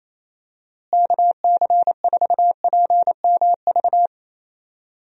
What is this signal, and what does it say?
Morse code, heard as a keyed tone, about 28 wpm, 700 Hz. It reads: KC4PMV
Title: USA Call Signs - 6 Characters - Repeated 3x - 28wpm